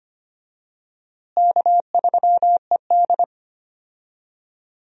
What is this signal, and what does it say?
Morse code, heard as a keyed tone, about 25 wpm, 700 Hz. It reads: K3ED
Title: USA Call Signs - 4 Characters - Repeated 3x - 25wpm